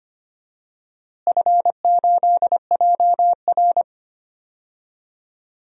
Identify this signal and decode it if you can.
Morse code, heard as a keyed tone, about 25 wpm, 700 Hz. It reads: F8JR
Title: DXCC-100 Call Signs - 4 Characters - Repeated 3x - 25wpm